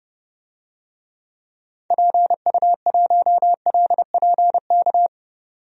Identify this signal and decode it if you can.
Morse code, heard as a keyed tone, about 30 wpm, 700 Hz. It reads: PU1LPK
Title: DXCC-100 Call Signs - 6 Characters - 30wpm